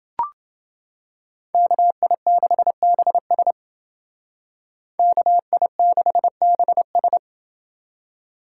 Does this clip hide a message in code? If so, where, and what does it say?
Yes: Morse code from 1.5 to 7.2 s, heard as a keyed tone, about 29 wpm, 700 Hz. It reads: KI6BS KI6BS